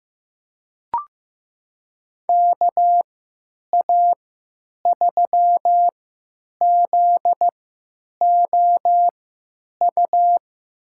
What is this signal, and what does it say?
Morse code, heard as a keyed tone, about 15 wpm, spaced as for 9 wpm, 700 Hz. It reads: KA3ZOU
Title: USA Call Signs - 6 Characters - 3x Character Spacing - 15wpm